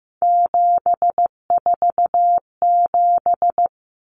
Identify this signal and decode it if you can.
Morse code, heard as a keyed tone, about 15 wpm, 700 Hz. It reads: 747
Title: Common Cut Numbers - 3-Digit 15wpm